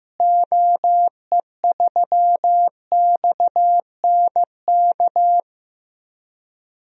Morse code, heard as a keyed tone, about 16 wpm, 700 Hz. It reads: OE3XNK
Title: DXCC-100 Call Signs - 6 Characters - Repeated 3x - 15wpm